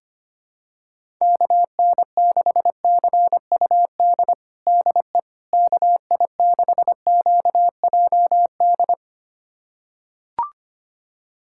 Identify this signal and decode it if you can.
Morse code, heard as a keyed tone, about 25 wpm, 700 Hz. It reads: KN6CUD DE KI6QJD
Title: USA Callsign DE Callsign - 6 Characters - 25wpm